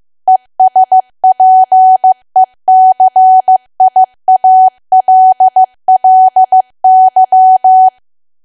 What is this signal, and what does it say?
Morse code, heard as a keyed tone, about 15 wpm, 750 Hz. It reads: ESPECIALLY
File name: cwa / W1AW-Level2 / 15WPM